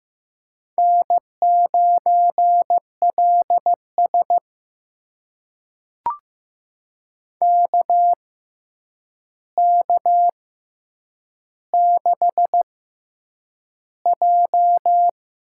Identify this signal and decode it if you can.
Morse code, heard as a keyed tone, about 15 wpm, spaced as for 8 wpm, 700 Hz. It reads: N9LS KK6J
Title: USA Call Signs - 4 Characters - 6x Character Spacing - 15wpm